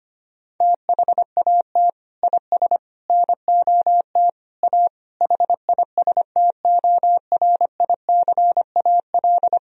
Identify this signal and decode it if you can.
Morse code, heard as a keyed tone, about 25 wpm, 700 Hz. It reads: THAT IS NOT A HISTORICAL
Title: Sentences from Top 1000 Words - Review Part 4 - 25wpm